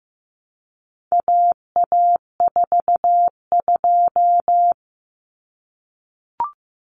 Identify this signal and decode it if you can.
Morse code, heard as a keyed tone, about 15 wpm, 700 Hz. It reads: AA42
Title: Common Cut Numbers - 4-Digit 15wpm